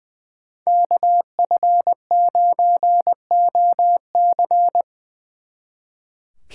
Morse code, heard as a keyed tone, about 24 wpm, 700 Hz. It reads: KF9OC
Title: Speed Racing - USA 2x2 Call Signs 20wpm